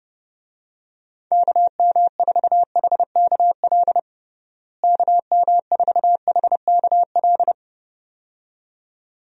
Morse code, heard as a keyed tone, about 32 wpm, 700 Hz. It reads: KM4HKL KM4HKL
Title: USA Call Signs - 6 Characters - Repeated 2x - 30wpm